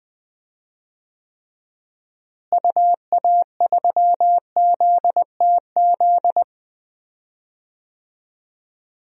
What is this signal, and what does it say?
Morse code, heard as a keyed tone, about 20 wpm, 700 Hz. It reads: UA3ZTZ
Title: DXCC-100 Call Signs - 6 Characters - Repeated 2x - 20wpm